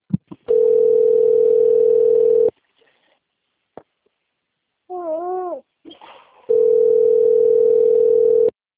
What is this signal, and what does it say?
Ringback tone (North America), 440 + 480 Hz.
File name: rg-1600-unknown- - - .13293